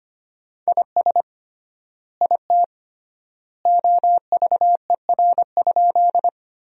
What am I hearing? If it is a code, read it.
Morse code, heard as a keyed tone, about 25 wpm, 700 Hz. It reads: IS IT OVER?
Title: 3x Word-Spacing - Sentences from Top 100 Words - 25wpm